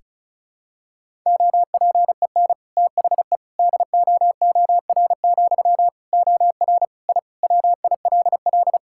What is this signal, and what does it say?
Morse code, heard as a keyed tone, about 36 wpm, 700 Hz. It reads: OPEN THE DOOR, OR I WILL
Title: Sentences from Top 1000 Words - Review Part 6 - 35wpm